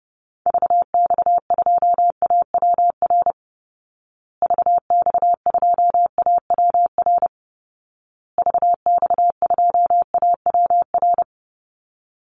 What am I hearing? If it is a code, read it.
Morse code, heard as a keyed tone, about 30 wpm, 700 Hz. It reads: VX2AWR VX2AWR VX2AWR
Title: Call Signs Repeated 2x - 30wpm